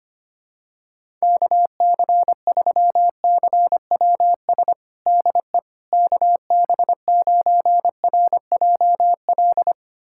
Morse code, heard as a keyed tone, about 25 wpm, 700 Hz. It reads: KC3CWS DE KB9RJL
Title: USA Callsign DE Callsign - 6 Characters - 25wpm